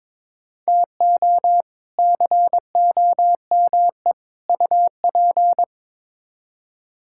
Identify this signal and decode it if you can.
Morse code, heard as a keyed tone, about 21 wpm, 700 Hz. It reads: TO COME UP
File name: practice / 22wpm / Build A Phrase - 5-Word Phrases - 22wpm